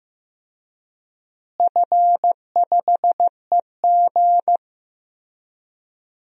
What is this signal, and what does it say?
Morse code, heard as a keyed tone, about 14 wpm, 700 Hz. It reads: F5EG
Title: DXCC-100 Call Signs - 4 Characters - Repeated 3x - 15wpm